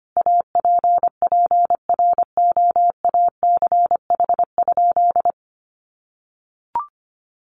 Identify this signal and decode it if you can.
Morse code, heard as a keyed tone, about 25 wpm, 700 Hz. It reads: APPROACH?
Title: Sentences from Top 1500 Words 25wpm